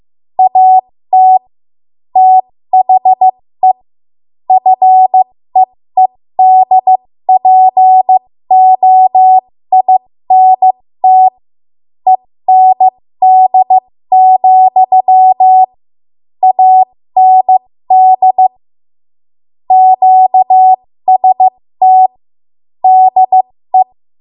Morse code, heard as a keyed tone, about 15 wpm, spaced as for 13 wpm, 750 Hz. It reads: AT THE FEEDPOINT END, AND QST DE
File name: Morse / 13WPM